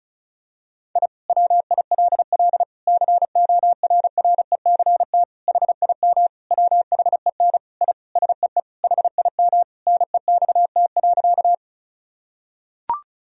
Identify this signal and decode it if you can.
Morse code, heard as a keyed tone, about 36 wpm, 700 Hz. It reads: I WILL CORRECT HIM WHEN I SEE HIM NEXT.
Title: Sentences from Top 1000 Words - Review Part 2 - 35wpm